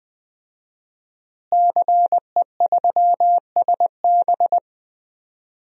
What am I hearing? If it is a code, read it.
Morse code, heard as a keyed tone, about 20 wpm, 700 Hz. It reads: CE3SB
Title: DXCC-100 Call Signs - 5 Characters - Repeated 2x - 20wpm